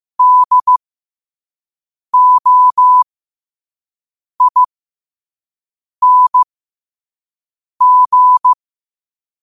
Morse code, heard as a keyed tone, about 15 wpm, spaced as for 5 wpm, 1000 Hz. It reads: DOING
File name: qotdfiles / 5wpm